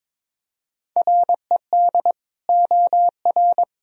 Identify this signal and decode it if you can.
Morse code, heard as a keyed tone, about 22 wpm, 700 Hz. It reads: RED OR
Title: Sentences from Top 1000 Words - Review Part 2 - 22wpm